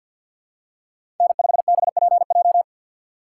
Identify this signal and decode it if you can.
Morse code, heard as a keyed tone, about 49 wpm, 700 Hz. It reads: N5BPJ
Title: USA Call Signs - 5 Characters - 6x Character Spacing - 50wpm